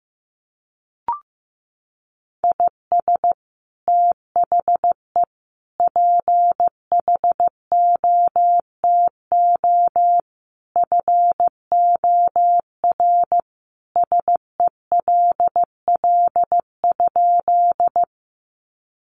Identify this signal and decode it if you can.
Morse code, heard as a keyed tone, about 15 wpm, 700 Hz. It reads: IS THE PHOTO FOR SELL?
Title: Sentences from Top 1200 Words 15wpm